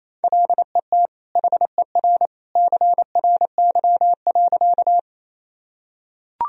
Morse code, heard as a keyed tone, about 28 wpm, 700 Hz. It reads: LET HER CRY.
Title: Sentences from Top 1000 Words - Review Part 2 - 28wpm